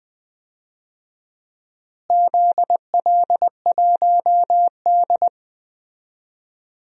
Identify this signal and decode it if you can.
Morse code, heard as a keyed tone, about 19 wpm, 700 Hz. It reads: ZL1D
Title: DXCC-100 Call Signs - 4 Characters - Repeated 3x - 20wpm